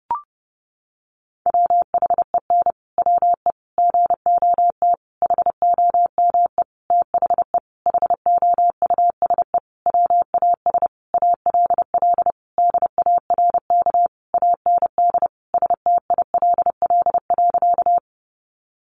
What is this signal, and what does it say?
Morse code, heard as a keyed tone, about 30 wpm, 700 Hz. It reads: WHEN WE GOT HOME THE HOUSE WAS ALL DARK AND STILL.
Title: Sentences from Top 500 Words 30wpm